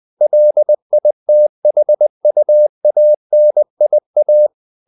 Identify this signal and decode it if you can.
Morse code, heard as a keyed tone, about 20 wpm, 600 Hz. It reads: LITHUANIA
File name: CW / Practice / NationalCapitals20